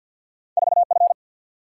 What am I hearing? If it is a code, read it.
Morse code, heard as a keyed tone, about 49 wpm, 700 Hz. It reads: 4F
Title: Single-Letter Single-Number 50wpm